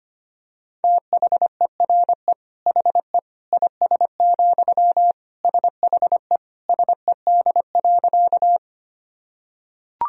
Morse code, heard as a keyed tone, about 25 wpm, 700 Hz. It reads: THERE HE IS, SHE SED.
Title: Sentences from Top 100 Words and Abbreviations 25wpm